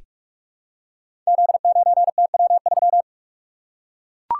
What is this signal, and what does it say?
Morse code, heard as a keyed tone, about 44 wpm, 700 Hz. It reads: Z9TW3
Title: Code Groups - 5 Letter:Numbers - 45wpm